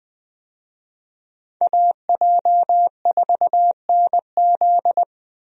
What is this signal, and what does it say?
Morse code, heard as a keyed tone, about 20 wpm, 700 Hz. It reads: AJ4NZ
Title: USA Call Signs - 5 Characters - Repeated 2x - 20wpm